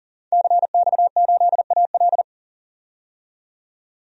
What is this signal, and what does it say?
Morse code, heard as a keyed tone, about 39 wpm, 700 Hz. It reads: CX8AL
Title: DXCC-100 Call Signs - 5 Characters - 40wpm